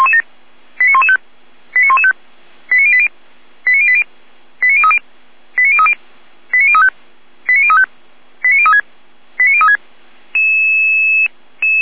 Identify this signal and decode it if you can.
Five-tone selective call (ZVEI), heard as quick runs of five tones, tones of about 70 ms. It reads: 89115 89115 89080 89080 89030 89030 89035 89035 89036 89036